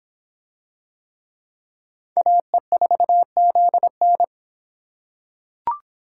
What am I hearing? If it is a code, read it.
Morse code, heard as a keyed tone, about 24 wpm, 700 Hz. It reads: AE4ZN